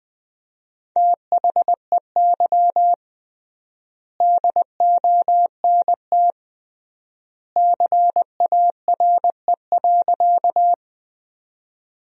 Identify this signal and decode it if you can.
Morse code, heard as a keyed tone, about 20 wpm, 700 Hz. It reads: THEY DONT CARE.
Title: 3x Word-Spacing - Sentences from Top 300 Words - 20wpm